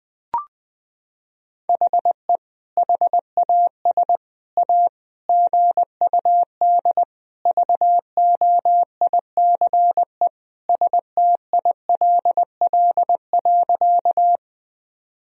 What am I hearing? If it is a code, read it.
Morse code, heard as a keyed tone, about 20 wpm, 700 Hz. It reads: HE HAS A GUD VOICE STILL.
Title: Sentences from Top 500 Words and Abbreviations 20wpm